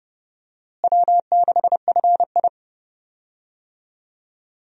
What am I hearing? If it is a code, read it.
Morse code, heard as a keyed tone, about 30 wpm, 700 Hz. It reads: W6FI
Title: USA Call Signs - 4 Characters - Repeated 3x - 30wpm